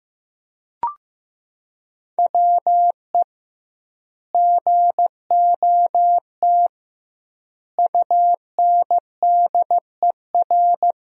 Morse code, heard as a keyed tone, about 15 wpm, 700 Hz. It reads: WE GOT UNDER
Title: 2x Word-Spacing - Sentences from Top 300 Words - 15wpm